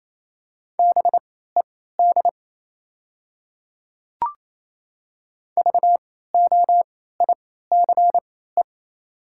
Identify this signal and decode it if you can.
Morse code, heard as a keyed tone, about 28 wpm, spaced as for 16 wpm, 690 Hz. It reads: BED VOICE
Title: 3x Character Spacing - Top 500 Words - 28wpm